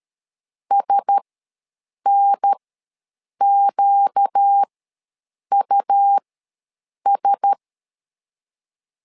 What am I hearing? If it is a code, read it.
Morse code, heard as a keyed tone, about 13 wpm, spaced as for 7 wpm, 800 Hz. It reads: SNQUS
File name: LEZIONI / 4 Q1S ws 12wpm 1827